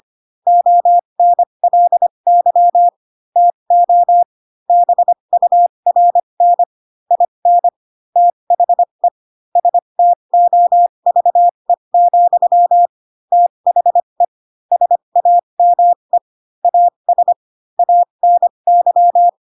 Morse code, heard as a keyed tone, about 25 wpm, spaced as for 22 wpm, 700 Hz. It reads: ONLY TO BURN IN THE STOVE, THE SAME AS ANY